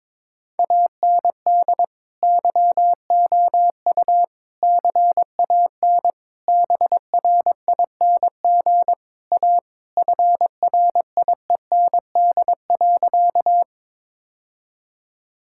Morse code, heard as a keyed tone, about 22 wpm, 700 Hz. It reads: AND YOU CAN BRING A FRIEND.
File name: practice / 22wpm / Sentences with 2-Word Phrases - 22wpm